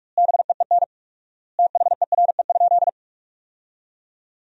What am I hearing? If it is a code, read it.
Morse code, heard as a keyed tone, about 46 wpm, 700 Hz. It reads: BEEN THERE?
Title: 4x Word-Spacing - Sentences from Top 100 Words - 45wpm